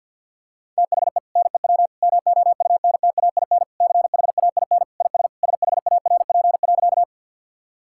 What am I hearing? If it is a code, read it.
Morse code, heard as a keyed tone, about 50 wpm, 700 Hz. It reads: THE NEW MOUNTAIN CHAIN IS SHARP.
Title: Sentences from Top 1600 Words 50wpm